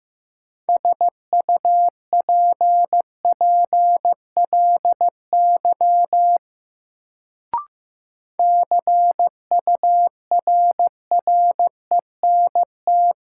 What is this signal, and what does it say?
Morse code, heard as a keyed tone, about 15 wpm, 700 Hz. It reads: SUPPLY CURRENT
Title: Sentences from Top 1000 Words - Review Part 4 - New Word Drill - 15wpm